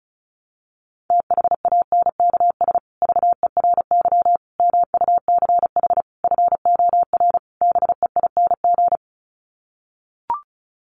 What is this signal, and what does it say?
Morse code, heard as a keyed tone, about 34 wpm, 700 Hz. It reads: THANKS VERY MUCH FOR BEING
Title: Sets of 5 Words Encore 35wpm